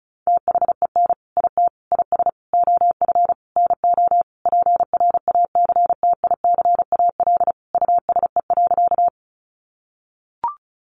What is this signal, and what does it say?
Morse code, heard as a keyed tone, about 34 wpm, 700 Hz. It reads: THEN IT IS OF NO PRACTICAL USE.
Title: Sentences from Top 1400 Words 35wpm